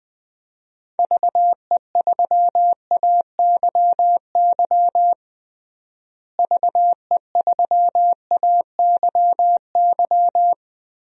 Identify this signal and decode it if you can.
Morse code, heard as a keyed tone, about 21 wpm, 700 Hz. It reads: VE3AYY VE3AYY
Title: DXCC-100 Call Signs - 6 Characters - Repeated 3x - 20wpm